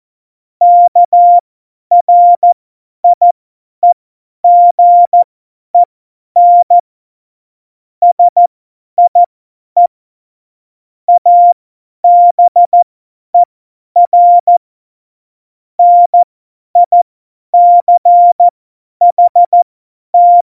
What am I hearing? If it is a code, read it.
Morse code, heard as a keyed tone, about 14 wpm, spaced as for 10 wpm, 700 Hz. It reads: KRIEGEN SIE ABER NICHT